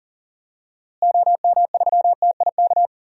Morse code, heard as a keyed tone, about 40 wpm, 700 Hz. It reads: OM3TIK
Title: DXCC-100 Call Signs - 6 Characters - Repeated 3x - 40wpm